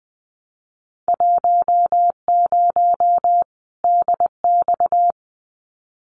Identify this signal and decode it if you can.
Morse code, heard as a keyed tone, about 20 wpm, 700 Hz. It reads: 10 DX
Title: Winter Field Day Exchanges 20wpm